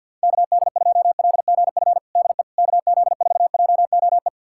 Morse code, heard as a keyed tone, about 51 wpm, 700 Hz. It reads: KD2LGF DE KZ4JOE
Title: USA Callsign DE Callsign - 6 Characters - 50wpm